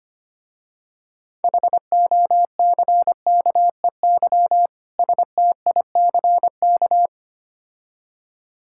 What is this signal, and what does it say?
Morse code, heard as a keyed tone, about 25 wpm, 700 Hz. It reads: HOCKEY STICK
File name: practice / 25wpm / Difficult Words with Hints - 25wpm